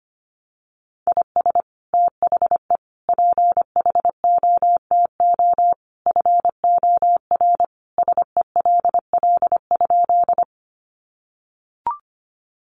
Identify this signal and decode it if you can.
Morse code, heard as a keyed tone, about 25 wpm, 700 Hz. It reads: IS THE PHOTO FOR SELL?
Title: Sentences from Top 1200 Words 25wpm